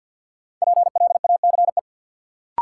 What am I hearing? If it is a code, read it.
Morse code, heard as a keyed tone, about 51 wpm, 700 Hz. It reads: PLACE
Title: Sentences from Top 200 Words - New Word Drill - 50wpm